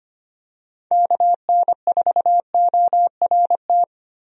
Morse code, heard as a keyed tone, about 25 wpm, 700 Hz. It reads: KN4ORT
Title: USA Call Signs - 6 Characters - Repeated 2x - 25wpm